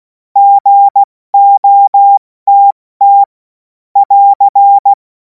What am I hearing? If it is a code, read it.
Morse code, heard as a keyed tone, about 16 wpm, 800 Hz. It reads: GOTT <AR>
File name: Morsekurs / Witze / Witz 24. März 2026 70 F80